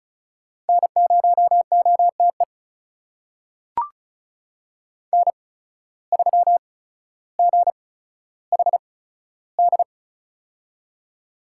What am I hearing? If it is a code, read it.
Morse code, heard as a keyed tone, about 36 wpm, spaced as for 15 wpm, 700 Hz. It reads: N0OTE N3GHD
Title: USA Call Signs - 5 Characters - 8x Character Spacing - 35wpm